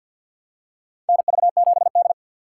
Morse code, heard as a keyed tone, about 46 wpm, 700 Hz. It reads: NV7D
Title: USA Call Signs - 4 Characters - 6x Character Spacing - 50wpm